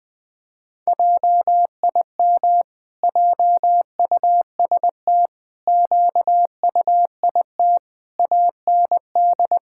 Morse code, heard as a keyed tone, about 21 wpm, 700 Hz. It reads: JIM JUST QUIT AND
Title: Pangram Challenge v2 - 20wpm